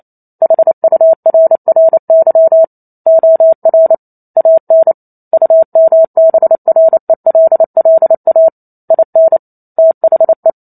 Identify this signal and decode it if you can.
Morse code, heard as a keyed tone, about 28 wpm, 650 Hz. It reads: HURRY OR AN UMBRELLA IN THE
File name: Chapter 2 - 28 WPM